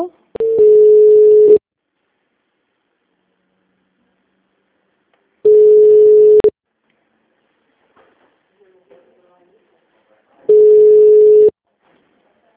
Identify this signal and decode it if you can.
Ringback tone (Europe, 425 Hz), with speech.
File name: out- -2007- - - .111